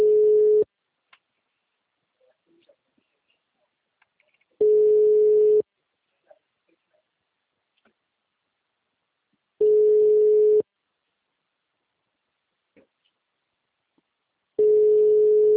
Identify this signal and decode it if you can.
Ringback tone (Europe, 425 Hz), with speech.